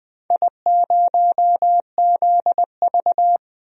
Morse code, heard as a keyed tone, about 20 wpm, 700 Hz. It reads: I0ZV
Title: DXCC-100 Call Signs - 4 Characters - 20wpm